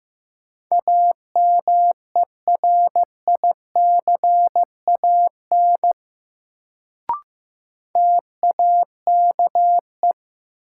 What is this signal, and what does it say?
Morse code, heard as a keyed tone, about 15 wpm, 700 Hz. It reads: AMERICAN TAKE
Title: Sets of 2 Words - New Word Drill - 15wpm